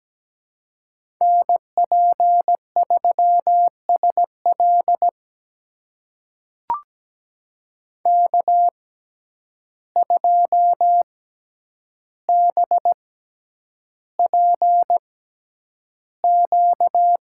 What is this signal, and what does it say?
Morse code, heard as a keyed tone, about 17 wpm, spaced as for 9 wpm, 700 Hz. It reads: NP3SL K2BPQ
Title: USA Call Signs - 5 Characters - 6x Character Spacing - 17wpm